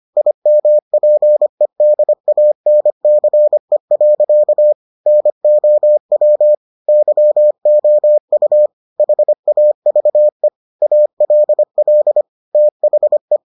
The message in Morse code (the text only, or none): IMPEDANCE. NOW YOU HAVE ALL THE